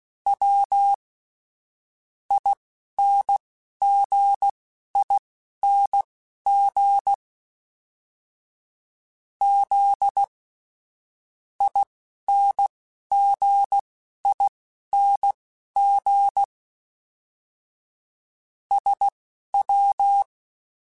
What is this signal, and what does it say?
Morse code, heard as a keyed tone, about 16 wpm, spaced as for 11 wpm, 800 Hz. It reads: W INGING Z INGING SW
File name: Special INGs / Special INGs-mp3s / Dble Ings sp @10